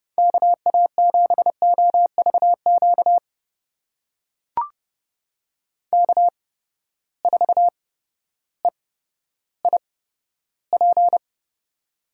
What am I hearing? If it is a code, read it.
Morse code, heard as a keyed tone, about 29 wpm, spaced as for 14 wpm, 700 Hz. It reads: KA7OVQ K4EIP
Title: USA Call Signs - 8x Character Spacing - 30wpm